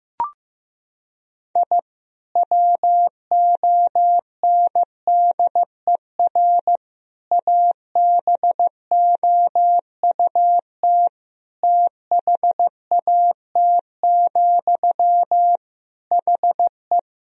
Morse code, heard as a keyed tone, about 15 wpm, 700 Hz. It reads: I WONDER ABOUT THAT, HE
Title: Sentences from Top 1000 Words - Review Part 3 - 15wpm